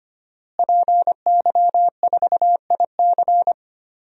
Morse code, heard as a keyed tone, about 25 wpm, 700 Hz. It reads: PY4IC
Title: DXCC-100 Call Signs - 5 Characters - Repeated 2x - 25wpm